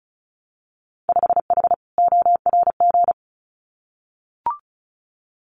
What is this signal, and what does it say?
Morse code, heard as a keyed tone, about 36 wpm, 700 Hz. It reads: 5H ORG
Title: Winter Field Day Exchanges Encore 35wpm